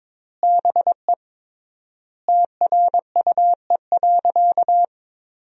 Morse code, heard as a keyed tone, about 22 wpm, 700 Hz. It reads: BE TRUE.
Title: 3x Word-Spacing - Sentences from Top 400 Words - 22wpm